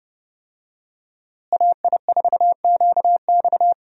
Morse code, heard as a keyed tone, about 32 wpm, 700 Hz. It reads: AI4QX
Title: USA Call Signs - 5 Characters - Repeated 2x - 30wpm